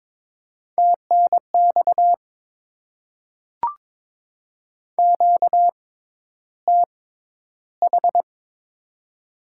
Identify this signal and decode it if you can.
Morse code, heard as a keyed tone, about 22 wpm, spaced as for 11 wpm, 700 Hz. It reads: TNX QTH
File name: Top 20 QSO Elements - 6x Character Spacing - 22wpm